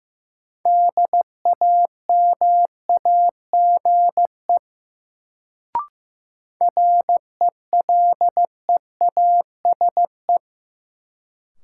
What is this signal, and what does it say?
Morse code, heard as a keyed tone, about 15 wpm, 700 Hz. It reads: DAMAGE RELEASE
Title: Sentences from Top 1100 Words - New Word Drill - 15wpm